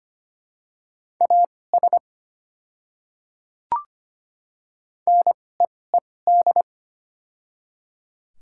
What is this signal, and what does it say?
Morse code, heard as a keyed tone, about 25 wpm, spaced as for 18 wpm, 700 Hz. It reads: AS NEED
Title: 2x Character Spacing - Top 200 Words - 25wpm